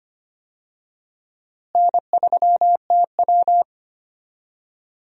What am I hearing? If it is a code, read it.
Morse code, heard as a keyed tone, about 25 wpm, 700 Hz. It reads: N3TW
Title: USA Call Signs - 4 Characters - Repeated 3x - 25wpm